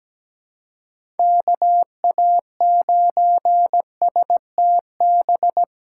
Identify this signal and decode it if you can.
Morse code, heard as a keyed tone, about 17 wpm, 700 Hz. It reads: KA9STB
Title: USA Call Signs - 6 Characters - 8x Character Spacing - 17wpm